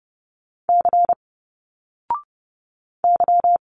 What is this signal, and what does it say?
Morse code, heard as a keyed tone, about 30 wpm, 700 Hz. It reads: C Y
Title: Single Letters 30wpm